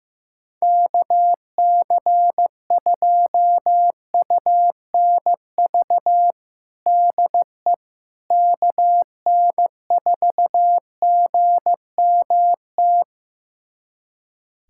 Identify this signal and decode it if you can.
Morse code, heard as a keyed tone, about 15 wpm, 700 Hz. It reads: KC2UNV DE KN4GMT
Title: USA Callsign DE Callsign - 6 Characters - 15wpm